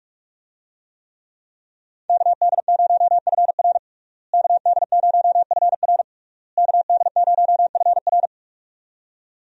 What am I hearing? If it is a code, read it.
Morse code, heard as a keyed tone, about 46 wpm, 700 Hz. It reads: KD0FR KD0FR KD0FR
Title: USA Call Signs - 5 Characters - Repeated 3x - 45wpm